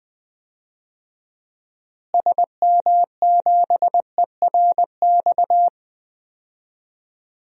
Morse code, heard as a keyed tone, about 20 wpm, 700 Hz. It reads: SM7ERX
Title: DXCC-100 Call Signs - 6 Characters - Repeated 3x - 20wpm